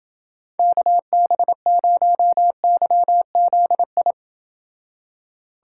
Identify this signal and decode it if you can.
Morse code, heard as a keyed tone, about 28 wpm, 700 Hz. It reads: KB0YZI